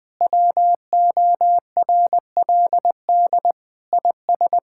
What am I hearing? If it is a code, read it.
Morse code, heard as a keyed tone, about 20 wpm, 700 Hz. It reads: WORLD IS
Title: Sentences with 4-Word Phrases - 20wpm